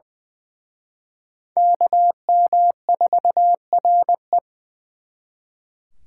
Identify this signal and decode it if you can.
Morse code, heard as a keyed tone, about 24 wpm, 700 Hz. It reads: KM4RE
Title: Speed Racing - USA 2x2 Call Signs 20wpm